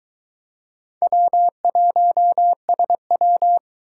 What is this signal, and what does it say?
Morse code, heard as a keyed tone, about 21 wpm, 700 Hz. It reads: W1SW